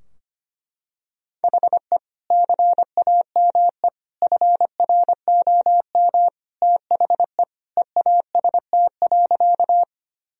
Morse code, heard as a keyed tone, about 25 wpm, 700 Hz. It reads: HE CAME FROM THE EAST.
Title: Sentences from Top 1000 Words - Review Part 3 - 25wpm